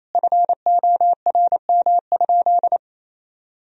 Morse code, heard as a keyed tone, about 28 wpm, 700 Hz. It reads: FORM?
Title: Sentences from Top 1000 Words - Review Part 1 - 28wpm